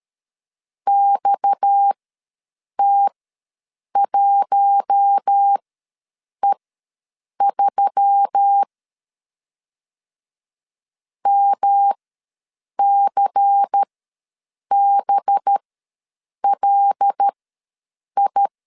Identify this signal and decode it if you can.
Morse code, heard as a keyed tone, about 13 wpm, spaced as for 7 wpm, 800 Hz. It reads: XT1E3 MCBLI
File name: LEZIONI / 6 PKX0 / 6 PKX ws 12wpm 3106